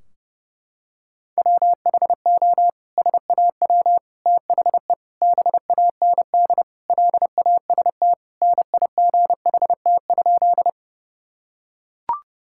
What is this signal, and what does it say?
Morse code, heard as a keyed tone, about 32 wpm, 700 Hz. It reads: WHO SAW THE BAND LAST NIGHT?
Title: Sentences from Top 1600 Words 30wpm